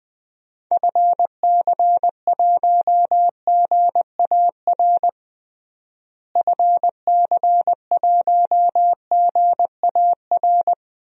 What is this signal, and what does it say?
Morse code, heard as a keyed tone, about 20 wpm, 700 Hz. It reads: FC1GAR FC1GAR
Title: DXCC-100 Call Signs - 6 Characters - Repeated 3x - 20wpm